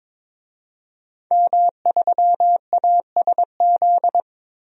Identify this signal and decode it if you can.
Morse code, heard as a keyed tone, about 22 wpm, 700 Hz. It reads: M3ASZ
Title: DXCC-100 Call Signs - 5 Characters - 22wpm